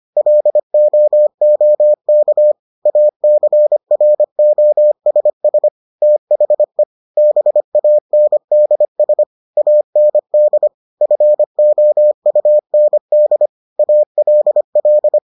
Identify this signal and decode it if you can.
Morse code, heard as a keyed tone, about 25 wpm, 600 Hz. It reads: LOOK ACROSS THE BANDS AND FOUND ALL